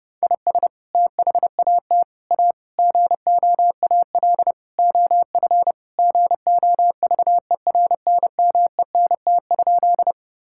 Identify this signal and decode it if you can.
Morse code, heard as a keyed tone, about 32 wpm, 700 Hz. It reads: IS THAT A GOAL OF GOVERNMENT?
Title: Sentences from Top 1000 Words - Review Part 5 - 30wpm